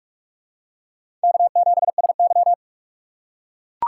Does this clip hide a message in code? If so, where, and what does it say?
Yes: Morse code from 1.2 to 2.5 s, heard as a keyed tone, about 44 wpm, 700 Hz. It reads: K7SY